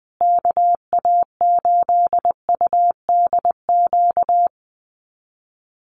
Morse code, heard as a keyed tone, about 19 wpm, 700 Hz. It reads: KA8UDQ
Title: USA Call Signs Repeated 3x 20wpm